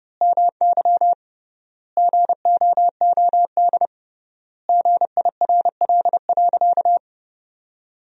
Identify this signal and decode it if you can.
Morse code, heard as a keyed tone, about 30 wpm, 700 Hz. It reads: MY GOOD GIRL.
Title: 3x Word-Spacing - Sentences from Top 400 Words - 30wpm